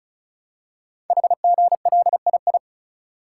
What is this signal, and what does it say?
Morse code, heard as a keyed tone, about 36 wpm, 690 Hz. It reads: HGLII
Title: Code Groups - 5 Letters - 35wpm